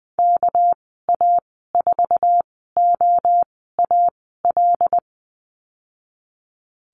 Morse code, heard as a keyed tone, about 20 wpm, spaced as for 15 wpm, 700 Hz. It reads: KA4OAL
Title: 207 Lesson 33g - Letter j - Call Signs 20wpm